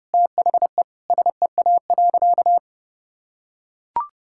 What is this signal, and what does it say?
Morse code, heard as a keyed tone, about 30 wpm, 700 Hz. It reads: THE SEA.
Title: Sentences from Top 1000 Words - Review Part 5 - 30wpm